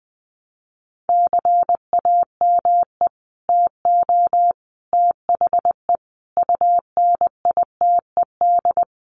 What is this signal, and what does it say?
Morse code, heard as a keyed tone, about 20 wpm, 700 Hz. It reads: CAME TO THE UNITED
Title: Sets of 5 Words Encore2 20wpm